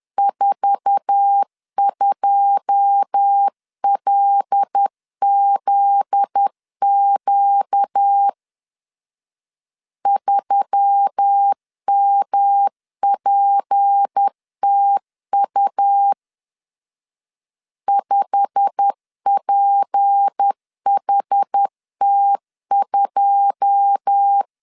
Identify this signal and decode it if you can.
Morse code, heard as a keyed tone, about 10 wpm, 800 Hz. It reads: 42LZQ 3MPTU 5PHT2
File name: LEZIONI / 9 2 10wpm 2736